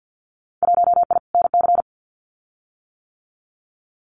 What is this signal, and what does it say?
Morse code, heard as a keyed tone, about 50 wpm, 700 Hz. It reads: 1I NC
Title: Winter Field Day Exchanges 50wpm